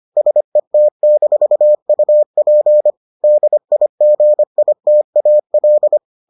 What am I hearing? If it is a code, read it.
Morse code, heard as a keyed tone, about 25 wpm, 600 Hz. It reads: SET-UP DIGITAL